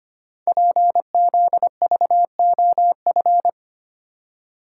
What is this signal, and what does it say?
Morse code, heard as a keyed tone, about 25 wpm, 700 Hz. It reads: PZVOF
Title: Code Groups - 5 Letters - 25wpm